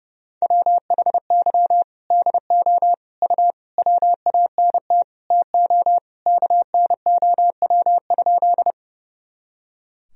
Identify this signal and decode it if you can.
Morse code, heard as a keyed tone, about 30 wpm, 700 Hz. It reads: WHY DO U WANT TO KNOW?
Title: Sentences from Top 200 Words and Abbreviations 30wpm